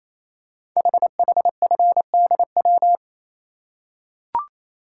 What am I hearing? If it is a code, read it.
Morse code, heard as a keyed tone, about 29 wpm, 700 Hz. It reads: HHFDW
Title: Code Groups - 5 Letters - 28wpm